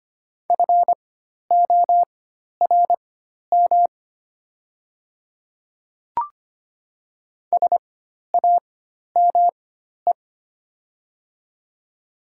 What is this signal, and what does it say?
Morse code, heard as a keyed tone, about 25 wpm, spaced as for 12 wpm, 700 Hz. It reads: FORM SAME
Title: 4x Character Spacing - Top 300 Words - 25wpm